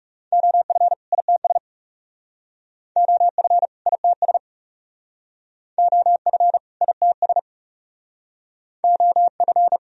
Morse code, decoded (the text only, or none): OF ITS OF ITS OF ITS OF